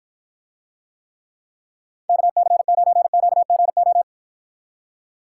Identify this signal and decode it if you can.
Morse code, heard as a keyed tone, about 54 wpm, 700 Hz. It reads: KC9QGO